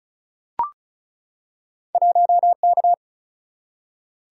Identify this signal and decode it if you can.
Morse code, heard as a keyed tone, about 36 wpm, 700 Hz. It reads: 1K
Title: Single-Letter Single-Number 35wpm